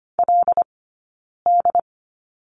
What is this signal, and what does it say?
Morse code, heard as a keyed tone, about 25 wpm, 700 Hz. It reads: L D
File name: Lesson 15d - Number 5 - Two Characters 25wpm